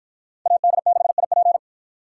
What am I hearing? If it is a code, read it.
Morse code, heard as a keyed tone, about 48 wpm, 700 Hz. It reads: AD6IP